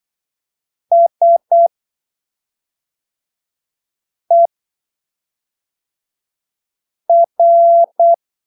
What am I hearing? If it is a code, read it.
Morse code, heard as a keyed tone, about 8 wpm, spaced as for 3 wpm, 680 Hz. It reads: SER
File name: arquivos / texto / texto 18 08ppm